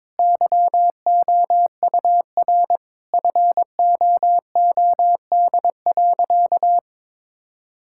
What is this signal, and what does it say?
Morse code, heard as a keyed tone, about 22 wpm, 700 Hz. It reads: YOUR FOOD.
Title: Sentences from Top 1000 Words - Review Part 5 - 22wpm